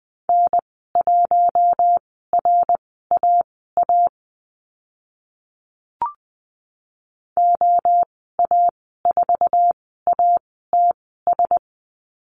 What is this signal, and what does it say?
Morse code, heard as a keyed tone, about 20 wpm, spaced as for 14 wpm, 700 Hz. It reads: N1RAA OA4ATS
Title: 076 Lesson 13g - Letter L - Call Signs 20wpm